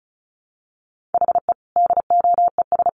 Morse code, heard as a keyed tone, about 34 wpm, 700 Hz. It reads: HE DOES
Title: Sets of 3 Words 35wpm